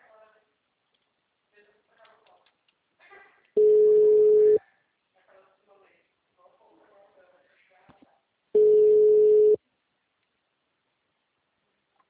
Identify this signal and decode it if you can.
Ringback tone (Europe, 425 Hz), with speech.